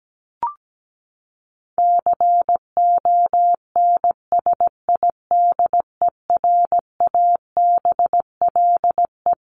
Morse code, heard as a keyed tone, about 17 wpm, 700 Hz. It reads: CONSIDERABLE
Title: LY Suffix - Top 100 Words with Base Words 17wpm